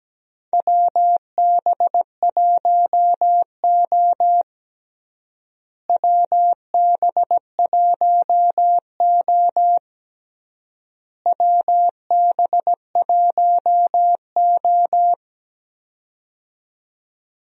Morse code, heard as a keyed tone, about 17 wpm, 700 Hz. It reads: WB1O WB1O WB1O
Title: USA Call Signs - 4 Characters - Repeated 3x - 17wpm